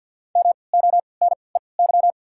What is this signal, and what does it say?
Morse code, heard as a keyed tone, about 53 wpm, spaced as for 29 wpm, 690 Hz. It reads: MONEY
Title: 3x Character Spacing - Top 500 Words - 50wpm